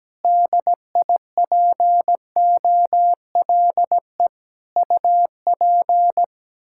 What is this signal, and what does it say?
Morse code, heard as a keyed tone, about 17 wpm, 700 Hz. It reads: DIPOLE UP
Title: Practice QSO - Exchange 2 - Info - 17wpm